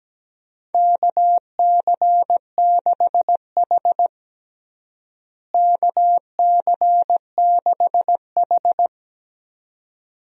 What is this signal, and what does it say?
Morse code, heard as a keyed tone, about 17 wpm, 700 Hz. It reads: KC6H KC6H
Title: USA Call Signs - 4 Characters - Repeated 2x - 17wpm